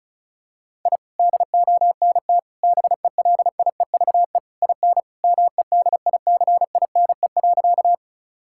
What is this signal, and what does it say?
Morse code, heard as a keyed tone, about 36 wpm, 700 Hz. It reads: I DONT BELIEVE IN MEDICINE.
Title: Sentences from Top 1000 Words - Review Part 6 - 35wpm